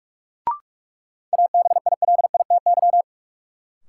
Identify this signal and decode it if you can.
Morse code, heard as a keyed tone, about 46 wpm, 700 Hz. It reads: ABILITY
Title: Sentences from Top 1200 Words - New Word Drill Alt - 45wpm